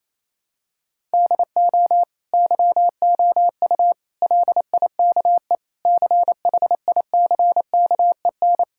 Morse code, heard as a keyed tone, about 28 wpm, 700 Hz. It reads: DO YOU LIKE CHICKEN
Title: Sentences from Top 1000 Words - Review Part 6 - 28wpm